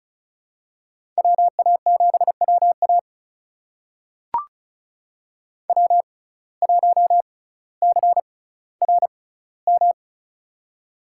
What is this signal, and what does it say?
Morse code, heard as a keyed tone, about 36 wpm, spaced as for 18 wpm, 700 Hz. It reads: WA7WA W1CRM
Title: USA Call Signs - 5 Characters - 6x Character Spacing - 35wpm